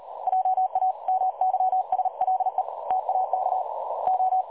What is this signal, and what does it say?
Morse code, heard as a keyed tone, about 38 wpm, 750 Hz. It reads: JA MYSLE TAK O
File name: dlazaawansowanych / nagraniazpasma / websdr recording 3565.0kHz